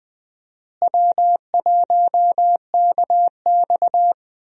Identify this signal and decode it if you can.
Morse code, heard as a keyed tone, about 20 wpm, 700 Hz. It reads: W1KX
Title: USA Call Signs - 4 Characters - Repeated 2x - 20wpm